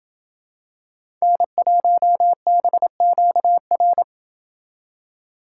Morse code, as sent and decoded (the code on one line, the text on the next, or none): -. .---- -... --.- .-.
N1BQR